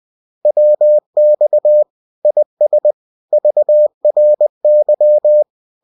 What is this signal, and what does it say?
Morse code, heard as a keyed tone, about 20 wpm, 600 Hz. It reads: WX IS VRY